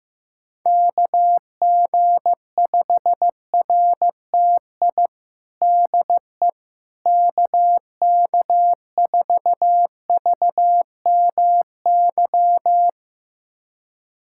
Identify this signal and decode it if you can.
Morse code, heard as a keyed tone, about 15 wpm, 700 Hz. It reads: KG5RTI DE KK4VMY
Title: USA Callsign DE Callsign 15wpm